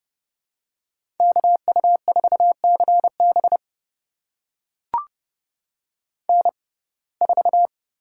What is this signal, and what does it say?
Morse code, heard as a keyed tone, about 31 wpm, spaced as for 22 wpm, 700 Hz. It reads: KU4CB N4
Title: USA Call Signs - 5 Characters - 6x Character Spacing - 30wpm